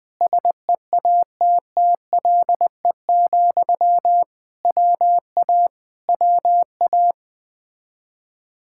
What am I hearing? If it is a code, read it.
Morse code, heard as a keyed tone, about 20 wpm, 700 Hz. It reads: SEATTLE, WA WA
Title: Speed-Racing Common QSO Phrases - 17wpm